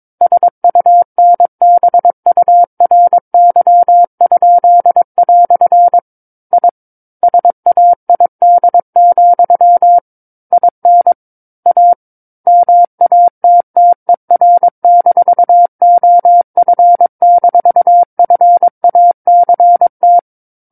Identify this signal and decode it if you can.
Morse code, heard as a keyed tone, about 22 wpm, 700 Hz. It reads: SUNBURY?" I SAID, IN A MATTER-OF-FACT
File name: Book One / 13-Chapter 13 - 22 WPM